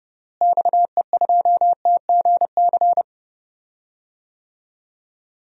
Morse code, heard as a keyed tone, about 30 wpm, 700 Hz. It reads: XE2TGC
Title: DXCC-100 Call Signs - 6 Characters - Repeated 3x - 30wpm